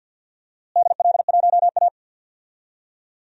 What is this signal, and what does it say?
Morse code, heard as a keyed tone, about 53 wpm, 700 Hz. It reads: DL1A
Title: DXCC-100 Call Signs - 4 Characters - Repeated 3x - 50wpm